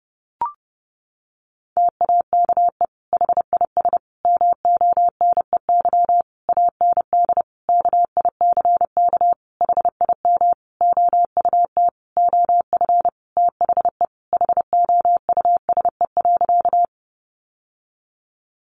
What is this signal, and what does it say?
Morse code, heard as a keyed tone, about 30 wpm, 700 Hz. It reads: TAKE HIS MONEY AND KICK HIM OUT OF THE HOUSE.
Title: Sentences from Top 1300 Words 30wpm